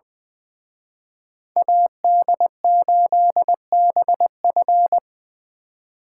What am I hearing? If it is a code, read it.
Morse code, heard as a keyed tone, about 24 wpm, 700 Hz. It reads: AD8BF
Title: Speed Racing - USA 2x2 Call Signs 20wpm